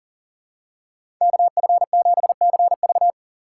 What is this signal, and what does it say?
Morse code, heard as a keyed tone, about 37 wpm, 700 Hz. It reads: KF7CV
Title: USA Call Signs - 4x Character Spacing - 40wpm